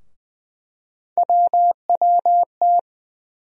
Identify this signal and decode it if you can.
Morse code, heard as a keyed tone, about 21 wpm, 700 Hz. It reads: WWT
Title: USA Call Sign Postfixes - 3 Letters 20wpm